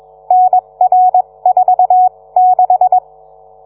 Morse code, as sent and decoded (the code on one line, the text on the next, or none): -. .-. ....- -....
NR46